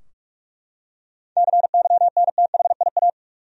Morse code, heard as a keyed tone, about 41 wpm, 700 Hz. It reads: CYNTHIA
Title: USA Common Names - 45wpm